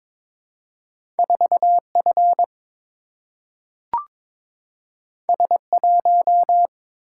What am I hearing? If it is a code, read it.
Morse code, heard as a keyed tone, about 22 wpm, 700 Hz. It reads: 4F S1
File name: practice / 22wpm / Two Letter-Number Combinations - 22wpm